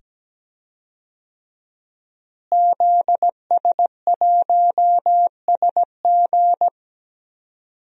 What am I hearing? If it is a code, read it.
Morse code, heard as a keyed tone, about 17 wpm, 700 Hz. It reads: ZS1SG
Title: DXCC-100 Call Signs - 5 Characters - Repeated 2x - 17wpm